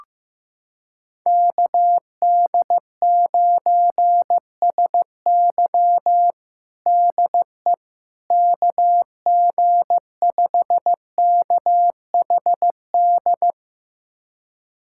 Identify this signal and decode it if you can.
Morse code, heard as a keyed tone, about 15 wpm, 700 Hz. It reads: KD9SY DE KG5KHD
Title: USA Callsign DE Callsign 15wpm